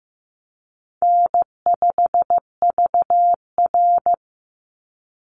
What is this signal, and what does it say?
Morse code, heard as a keyed tone, about 15 wpm, 700 Hz. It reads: N5VR
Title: Easy Call Signs Repeated 3x - 15wpm